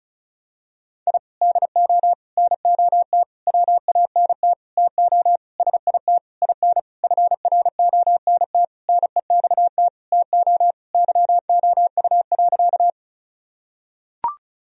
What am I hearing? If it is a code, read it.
Morse code, heard as a keyed tone, about 34 wpm, 700 Hz. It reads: I DO NOT WANT TO SIT IN FRONT NEXT TO YOU.
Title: Sentences from Top 1000 Words - Review Part 2 - 35wpm